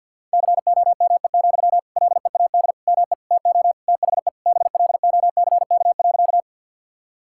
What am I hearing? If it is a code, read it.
Morse code, heard as a keyed tone, about 49 wpm, 700 Hz. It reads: COME, LEAD ME TO THE BLOCK.
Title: Sentences from Top 1600 Words 50wpm